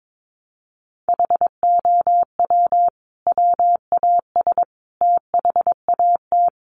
Morse code, heard as a keyed tone, about 22 wpm, 700 Hz. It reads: HOW WAS THAT
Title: Sentences from Top 500 Words 22wpm